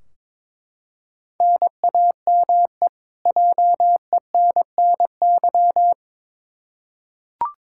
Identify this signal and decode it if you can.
Morse code, heard as a keyed tone, about 22 wpm, 700 Hz. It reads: NAME JENNY
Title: K1USN SST QSO Practice 22wpm